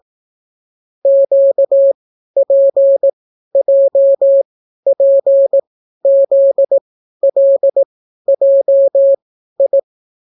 Morse code, heard as a keyed tone, about 18 wpm, spaced as for 12 wpm, 550 Hz. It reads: QPJPZLJI